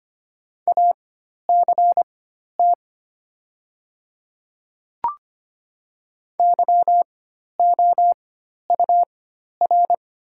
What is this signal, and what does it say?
Morse code, heard as a keyed tone, about 25 wpm, spaced as for 12 wpm, 700 Hz. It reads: ACT YOUR
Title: 4x Character Spacing - Top 200 Words - 25wpm